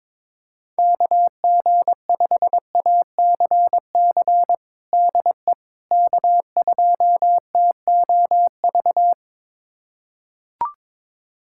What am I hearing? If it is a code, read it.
Morse code, heard as a keyed tone, about 22 wpm, 700 Hz. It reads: KG5ACC DE K2TOV
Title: USA Callsign DE Callsign 22wpm